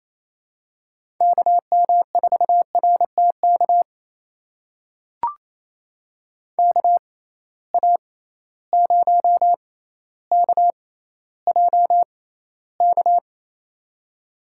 Morse code, heard as a keyed tone, about 28 wpm, spaced as for 14 wpm, 700 Hz. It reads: KM4RTK KA0KJK
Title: USA Call Signs - 6 Characters - 6x Character Spacing - 28wpm